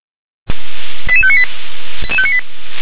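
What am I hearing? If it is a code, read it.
Five-tone selective call (ZVEI), heard as quick runs of five tones, tones of about 70 ms. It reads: 88598
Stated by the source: ZVEI five-tone selective call received on SDR.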